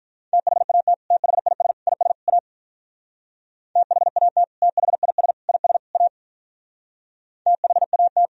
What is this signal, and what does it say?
Morse code, heard as a keyed tone, about 43 wpm, 700 Hz. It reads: THAT THIS IS A THAT THIS IS A THAT
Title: Speed-Racing - Sets of 4 Words 35wpm